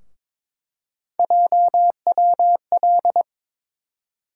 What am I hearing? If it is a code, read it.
Morse code, heard as a keyed tone, about 22 wpm, 700 Hz. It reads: JWL
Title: USA Call Sign Postfixes - 3 Letters 22wpm